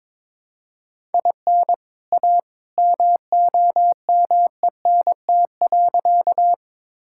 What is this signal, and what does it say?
Morse code, heard as a keyed tone, about 22 wpm, 700 Hz. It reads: IN A MOMENT.
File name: practice / 22wpm / Sentences with 2-Word Phrases - 22wpm